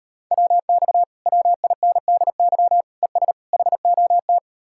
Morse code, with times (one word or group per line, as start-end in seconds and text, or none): 0.3-1.0 WX
1.3-2.8 WINDY
3.0-3.3 ES
3.5-4.4 HOT